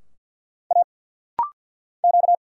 Morse code, heard as a keyed tone, about 50 wpm, 690 Hz. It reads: A Q
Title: Single Letter-Number Alt - Rapid-Fire 50wpm